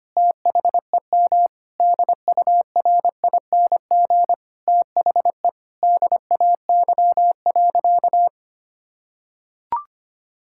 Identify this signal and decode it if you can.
Morse code, heard as a keyed tone, about 25 wpm, 700 Hz. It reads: THEM DURING THE DAY.
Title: Sentences from Top 1000 Words - Review Part 5 - 25wpm